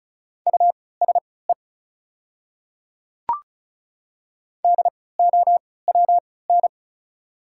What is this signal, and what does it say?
Morse code, heard as a keyed tone, about 33 wpm, spaced as for 20 wpm, 690 Hz. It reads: USE DOWN
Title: 3x Character Spacing - Top 300 Words - 35wpm